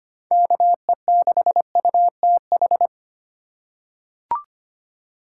Morse code, heard as a keyed tone, about 25 wpm, 700 Hz. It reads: KE6UTH
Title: USA Call Signs - 6 Characters - 4x Character Spacing - 25wpm